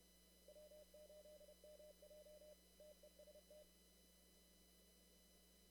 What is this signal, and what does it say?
Morse code, heard as a keyed tone, about 30 wpm, 600 Hz, with noise. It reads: W8MJ TEST
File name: CW / kt5j wpx cw-op-k5pi